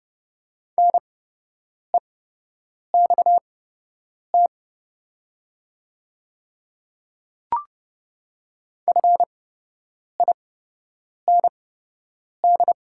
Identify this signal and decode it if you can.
Morse code, heard as a keyed tone, about 30 wpm, spaced as for 8 wpm, 700 Hz. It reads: NEXT FIND
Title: 8x Character Spacing - Top 300 Words - 30wpm